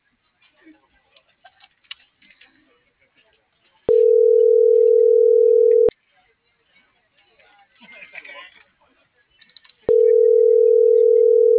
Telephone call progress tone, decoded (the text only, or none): ringback tone